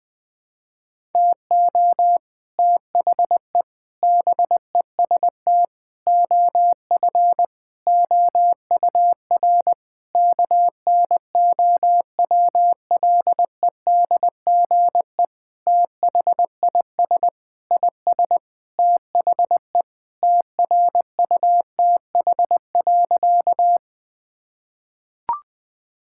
Morse code, heard as a keyed tone, about 20 wpm, 700 Hz. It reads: TO THE BEST OF OUR KNOWLEDGE THIS IS THE TRUTH.
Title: Sentences with 9-Word Phrases - Spoken Beginning - 20wpm